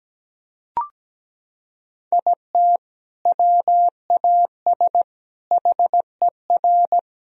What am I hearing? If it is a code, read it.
Morse code, heard as a keyed tone, about 17 wpm, 700 Hz. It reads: IT WAS HER
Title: Sentences from Top 1000 Words - Review Part 4 - 17wpm